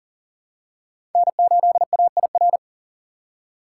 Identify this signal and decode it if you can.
Morse code, heard as a keyed tone, about 38 wpm, 700 Hz. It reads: N8AIR
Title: USA Call Signs - 5 Characters - 3x Character Spacing - 40wpm